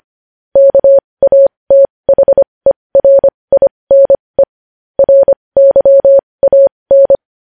Morse code, heard as a keyed tone, about 25 wpm, spaced as for 20 wpm, 550 Hz. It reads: KATHERINE RYAN
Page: news-headlines-in-morse-code-at-20-wpm